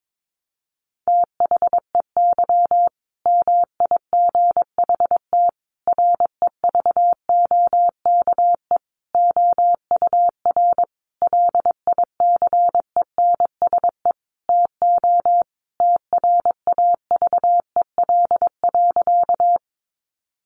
Morse code, heard as a keyed tone, about 22 wpm, 700 Hz. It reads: THEY MIGHT REVOKE OUR LICENSE TO TRAVEL.
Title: A Sound Of Thunder 22wpm